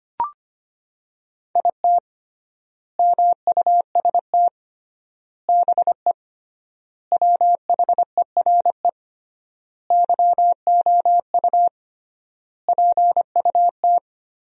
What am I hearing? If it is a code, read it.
Morse code, heard as a keyed tone, about 25 wpm, 700 Hz. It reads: IT MUST BE WHERE YOU PUT
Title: 3x Word-Spacing - Sentences from Top 200 Words - 25wpm